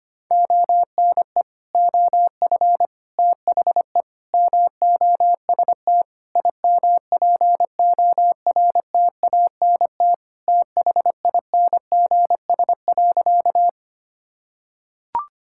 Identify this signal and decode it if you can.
Morse code, heard as a keyed tone, about 25 wpm, 700 Hz. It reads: ONE OF THE MOST IMPORTANT THINGS.
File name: practice / 25wpm / Sentences with 4-Word Phrases - 25wpm